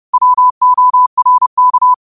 Morse code, heard as a keyed tone, about 30 wpm, 1000 Hz. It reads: WORK
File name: qotdfiles / 30wpm